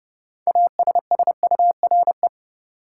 Morse code, heard as a keyed tone, about 30 wpm, 700 Hz. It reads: ASSURE
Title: Six Letter Words - 30wpm